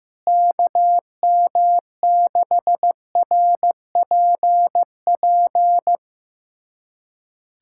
Morse code, heard as a keyed tone, about 15 wpm, 700 Hz. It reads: KM6RPP
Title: USA Callsign DE Callsign - 6 Characters - 15wpm